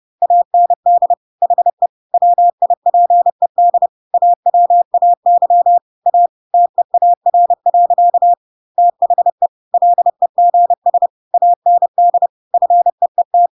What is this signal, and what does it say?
Morse code, heard as a keyed tone, about 30 wpm, 700 Hz. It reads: AND HE WIPED AWAY A TEAR. THE LEGS AND FEET
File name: Pinocchio / Pinocchio ch3 30 30 ik1hge